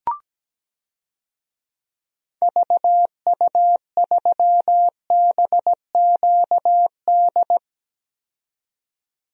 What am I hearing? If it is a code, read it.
Morse code, heard as a keyed tone, about 17 wpm, 700 Hz. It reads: VU3BQD
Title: DXCC-100 Call Signs - 6 Characters - 17wpm